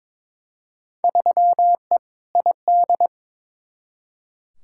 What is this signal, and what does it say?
Morse code, heard as a keyed tone, about 22 wpm, 700 Hz. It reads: 3E ID
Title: Field Day Exchanges Encore - 22wpm